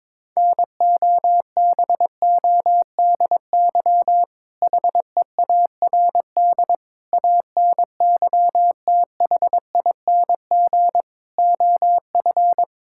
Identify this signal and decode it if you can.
Morse code, heard as a keyed tone, about 22 wpm, 700 Hz. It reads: NOBODY HEARD ANYTHING OF
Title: Sentences from Top 1600 Words 22wpm